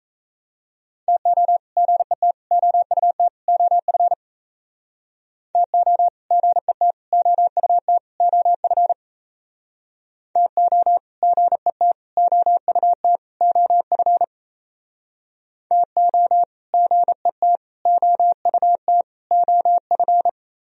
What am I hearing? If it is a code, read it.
Morse code, heard as a keyed tone, about 36 wpm, 690 Hz. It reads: TO GET OUT OF TO GET OUT OF TO GET OUT OF TO GET OUT OF